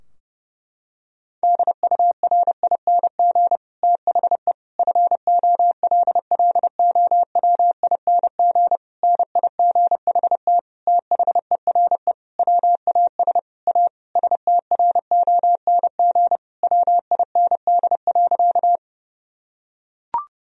Morse code, heard as a keyed tone, about 30 wpm, 700 Hz. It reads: DURING THE FOLLOWING NIGHT THERE WAS A STRONG WIND.
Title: Sentences from Top 1000 Words - Review Part 4 - 30wpm